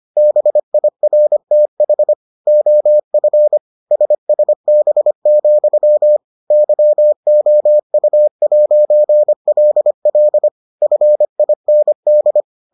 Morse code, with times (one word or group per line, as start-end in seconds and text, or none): 0.2-2.1 BIRTH
2.5-3.6 OF
3.9-6.2 SSB,
6.5-10.5 YOU'LL
10.8-12.4 FIND